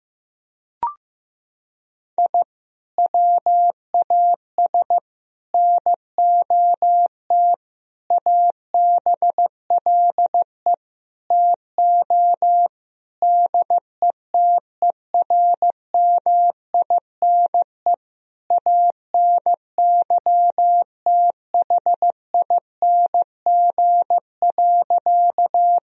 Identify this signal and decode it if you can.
Morse code, heard as a keyed tone, about 16 wpm, 700 Hz. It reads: I WAS NOT ABLE TO DETERMINE ANYTHING.
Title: Sentences from Top 1000 Words - Review Part 4 - 15wpm